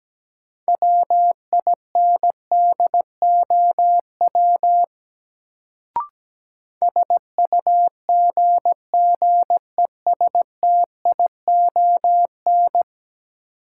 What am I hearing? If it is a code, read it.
Morse code, heard as a keyed tone, about 17 wpm, 700 Hz. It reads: WINDOW SUGGESTION
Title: Sentences from Top 1600 Words - New Word Drill Alt - 17wpm